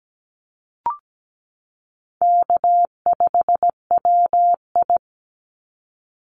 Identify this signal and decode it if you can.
Morse code, heard as a keyed tone, about 17 wpm, 700 Hz. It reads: K5WI
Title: NA SOTA Chaser Callsigns 17wpm